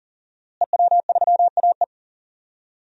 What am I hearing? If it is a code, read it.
Morse code, heard as a keyed tone, about 40 wpm, 700 Hz. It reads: EW3AE
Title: DXCC-100 Call Signs - 5 Characters - Repeated 2x - 40wpm